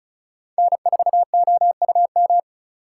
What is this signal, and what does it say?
Morse code, heard as a keyed tone, about 36 wpm, 700 Hz. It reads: N4OUM
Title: USA Call Signs - 4x Character Spacing - 35wpm